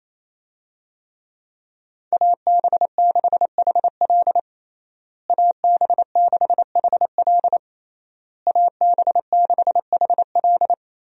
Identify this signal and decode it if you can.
Morse code, heard as a keyed tone, about 27 wpm, 700 Hz. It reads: AB6HL AB6HL AB6HL
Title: USA Call Signs - 5 Characters - Repeated 3x - 28wpm